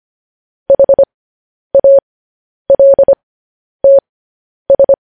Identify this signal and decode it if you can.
Morse code, heard as a keyed tone, about 25 wpm, spaced as for 10 wpm, 550 Hz. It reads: HALTS
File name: podcast / News headlines-1x-25x10